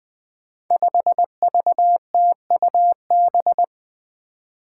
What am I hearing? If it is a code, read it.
Morse code, heard as a keyed tone, about 20 wpm, 700 Hz. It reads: 5VTUB
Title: Five Challenging Letter-Number Combinations 20wpm